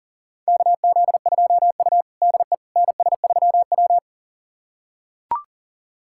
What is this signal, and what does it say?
Morse code, heard as a keyed tone, about 37 wpm, 700 Hz. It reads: KZ2U DE NS3W
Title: USA Callsign DE Callsign - 4 Characters - 40wpm